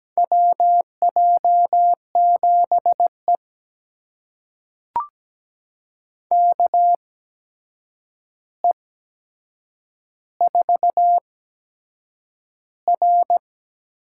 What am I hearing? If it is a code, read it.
Morse code, heard as a keyed tone, about 17 wpm, spaced as for 7 wpm, 700 Hz. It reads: WJ7E KE4R
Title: USA Call Signs - 8x Character Spacing - 17wpm